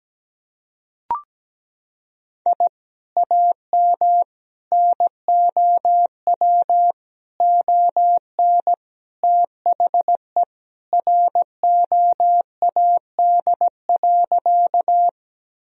Morse code, heard as a keyed tone, about 17 wpm, 700 Hz. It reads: I AM NOW ON THE ROAD.
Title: Sentences from Top 1000 Words - Review Part 2 - 17wpm